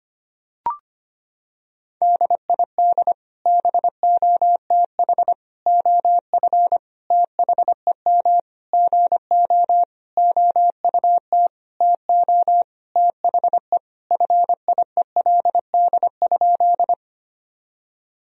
Morse code, heard as a keyed tone, about 25 wpm, 700 Hz. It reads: DID BOTH OF THEM GO OUT TO THE FIELD?
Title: Sentences from Top 1000 Words - Review Part 2 - 25wpm